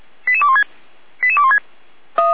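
Five-tone selective call (ZVEI), heard as quick runs of five tones, tones of about 80 ms. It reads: 88216 88216
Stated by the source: ZVEI five-tone selective call 88200 received on an SDR.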